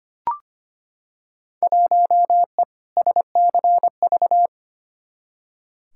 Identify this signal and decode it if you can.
Morse code, heard as a keyed tone, about 25 wpm, 700 Hz. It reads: 1E SCV
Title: Field Day Exchanges - 25wpm